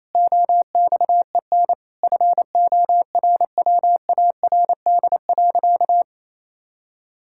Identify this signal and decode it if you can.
Morse code, heard as a keyed tone, about 28 wpm, 700 Hz. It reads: OXEN FORWARD.
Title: Pangram Challenge v2 - 28wpm